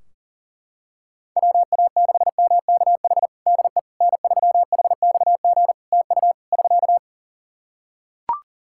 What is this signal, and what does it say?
Morse code, heard as a keyed tone, about 40 wpm, 700 Hz. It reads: WA6MKH DE N3HXG TU <SK>
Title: Practice QSO - Exchange 4 - Ending - 40wpm